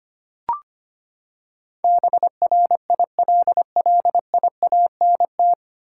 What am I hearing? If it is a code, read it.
Morse code, heard as a keyed tone, about 25 wpm, 700 Hz. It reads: BRILLIANT
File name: Nine Letter Words - 25wpm